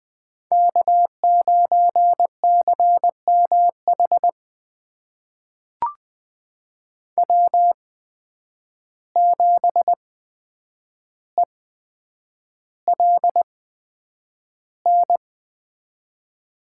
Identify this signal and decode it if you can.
Morse code, heard as a keyed tone, about 21 wpm, spaced as for 9 wpm, 700 Hz. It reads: K9CMH W7ELN
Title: USA Call Signs - 5 Characters - 8x Character Spacing - 20wpm